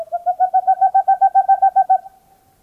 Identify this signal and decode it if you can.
There is no code.